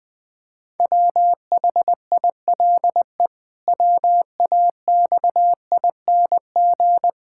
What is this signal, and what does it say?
Morse code, heard as a keyed tone, about 20 wpm, 700 Hz. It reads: WHILE WAXING
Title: Pangram Challenge v2 - 20wpm